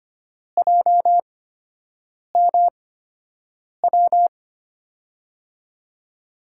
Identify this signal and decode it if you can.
Morse code, heard as a keyed tone, about 25 wpm, spaced as for 7 wpm, 700 Hz. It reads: JMW
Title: USA Call Signs - 6 Characters - 8x Character Spacing - 25wpm